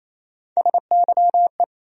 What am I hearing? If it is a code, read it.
Morse code, heard as a keyed tone, about 28 wpm, 700 Hz. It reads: SYE
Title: USA Call Sign Postfixes - 3 Letters 28wpm